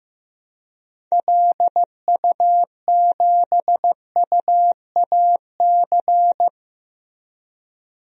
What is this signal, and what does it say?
Morse code, heard as a keyed tone, about 15 wpm, 700 Hz. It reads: LU7UAC
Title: DXCC-100 Call Signs - 6 Characters - Repeated 3x - 15wpm